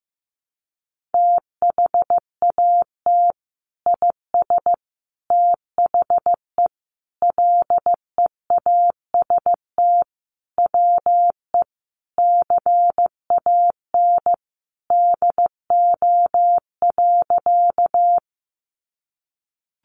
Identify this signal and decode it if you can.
Morse code, heard as a keyed tone, about 14 wpm, 700 Hz. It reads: THAT IS THE LEAST WE CAN DO.
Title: Sentences from Top 700 Words 15wpm